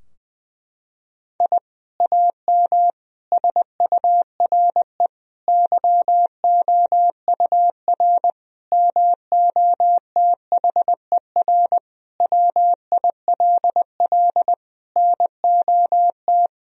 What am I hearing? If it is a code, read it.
Morse code, heard as a keyed tone, about 21 wpm, 700 Hz. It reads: I AM SURE YOUR MOTHER WILL NOT
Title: Sentences from Top 1000 Words - Review Part 3 - 20wpm